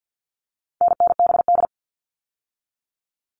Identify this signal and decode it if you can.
Morse code, heard as a keyed tone, about 48 wpm, 700 Hz. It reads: NNBD
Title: All Cut Numbers - 4-Digit 50wpm